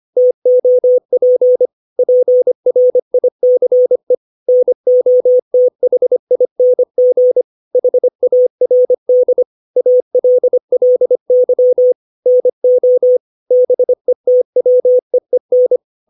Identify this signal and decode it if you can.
Morse code, heard as a keyed tone, about 25 wpm, 500 Hz. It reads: TOP PRICE NOTHING HARD ALLY NO BETWEEN
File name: 632 words / 632 Words 25WPM